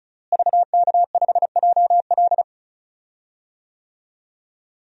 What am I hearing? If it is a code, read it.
Morse code, heard as a keyed tone, about 34 wpm, 700 Hz. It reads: VK5JL
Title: DXCC-100 Call Signs - 5 Characters - Repeated 3x - 35wpm